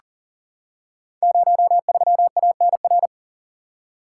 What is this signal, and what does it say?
Morse code, heard as a keyed tone, about 42 wpm, 700 Hz. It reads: 03ANR
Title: Code Groups - 5 Letter:Numbers - 40wpm